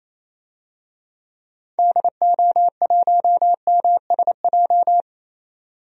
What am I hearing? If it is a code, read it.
Morse code, heard as a keyed tone, about 28 wpm, 700 Hz. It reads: DO1MSJ
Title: DXCC-100 Call Signs - 6 Characters - 28wpm